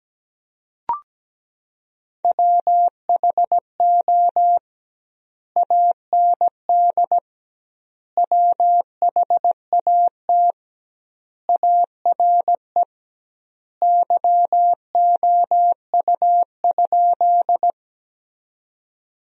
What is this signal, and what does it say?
Morse code, heard as a keyed tone, about 17 wpm, 700 Hz. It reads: WHO AND WHAT ARE YOU?
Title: 2x Word-Spacing - Sentences from Top 100 Words - 17wpm